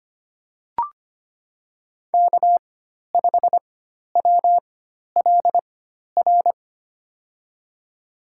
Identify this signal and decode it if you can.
Morse code, heard as a keyed tone, about 25 wpm, spaced as for 12 wpm, 700 Hz. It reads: K5WLR
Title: USA Call Signs - 5 Characters - 4x Character Spacing - 25wpm